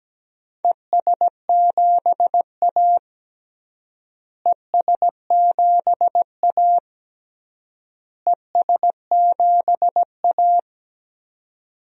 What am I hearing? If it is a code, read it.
Morse code, heard as a keyed tone, about 17 wpm, 700 Hz. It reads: ES7A ES7A ES7A
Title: DXCC-100 Call Signs - 4 Characters - Repeated 3x - 17wpm